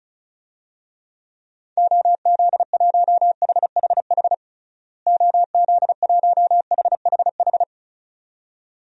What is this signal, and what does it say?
Morse code, heard as a keyed tone, about 36 wpm, 700 Hz. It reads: OZ1HHH OZ1HHH
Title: DXCC-100 Call Signs - 6 Characters - Repeated 2x - 35wpm